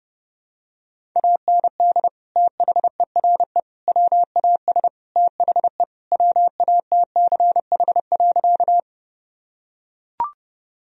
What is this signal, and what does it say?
Morse code, heard as a keyed tone, about 30 wpm, 700 Hz. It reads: AND THERE WAS THE WATCH.
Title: Sentences from Top 1000 Words - Review Part 1 - 30wpm